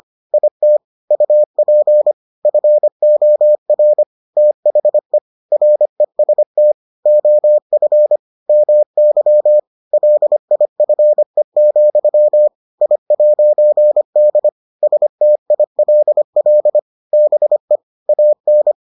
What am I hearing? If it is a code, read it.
Morse code, heard as a keyed tone, about 25 wpm, 600 Hz. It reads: IT UP FOR THE REST OF MY LIFE, I'D STILL BE AN